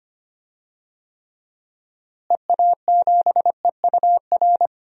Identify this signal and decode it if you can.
Morse code, heard as a keyed tone, about 25 wpm, 700 Hz. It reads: EA7EUR
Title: DXCC-100 Call Signs - 6 Characters - Repeated 2x - 25wpm